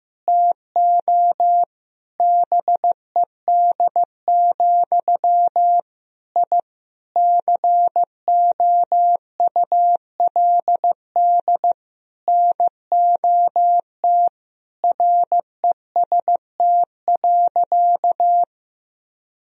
Morse code, heard as a keyed tone, about 16 wpm, 700 Hz. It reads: TO BED, I COULD NOT REST.
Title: Sentences from Top 1000 Words - Review Part 2 - 15wpm